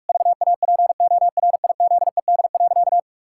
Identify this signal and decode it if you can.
Morse code, heard as a keyed tone, about 46 wpm, 700 Hz. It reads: VAPORIZED.
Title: Pangram Challenge v2 - 45wpm